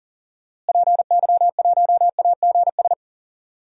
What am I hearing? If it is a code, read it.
Morse code, heard as a keyed tone, about 39 wpm, 700 Hz. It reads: PY1AGS
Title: DXCC-100 Call Signs - 6 Characters - Repeated 2x - 40wpm